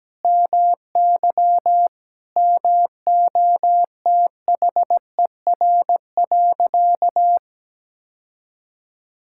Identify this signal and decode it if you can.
Morse code, heard as a keyed tone, about 17 wpm, 700 Hz. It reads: MY MOTHER.
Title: Sentences from Top 1600 Words 17wpm